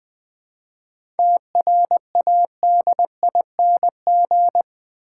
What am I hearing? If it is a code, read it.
Morse code, heard as a keyed tone, about 20 wpm, 700 Hz. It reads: TRADING
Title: 7 Letter Words - First 3 Letters Given - 20wpm